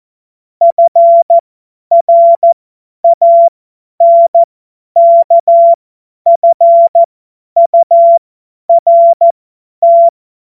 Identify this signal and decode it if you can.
Morse code, heard as a keyed tone, about 14 wpm, spaced as for 10 wpm, 680 Hz. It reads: FRANKFURT